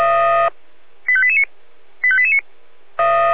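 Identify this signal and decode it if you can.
Five-tone selective call (ZVEI), heard as quick runs of five tones, tones of about 70 ms. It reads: 85889 85889
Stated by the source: ZVEI five-tone selective call received via SDR.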